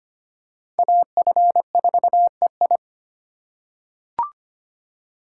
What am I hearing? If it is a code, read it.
Morse code, heard as a keyed tone, about 25 wpm, 700 Hz. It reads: AF4EI
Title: USA Call Signs - 5 Characters - 4x Character Spacing - 25wpm